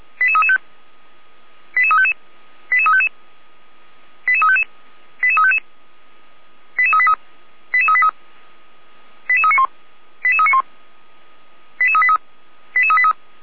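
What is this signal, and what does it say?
Five-tone selective call (ZVEI), heard as quick runs of five tones, tones of about 70 ms. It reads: 88335 88355 88355 88355 88355 88383 88383 88391 88391 88383 88383
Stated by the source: ZVEI five-tone call received on an SDR, from feuerwehr-orlamuende.org.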